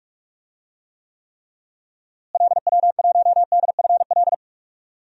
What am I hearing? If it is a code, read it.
Morse code, heard as a keyed tone, about 46 wpm, 700 Hz. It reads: LW1DFL
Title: DXCC-100 Call Signs - 6 Characters - Repeated 3x - 45wpm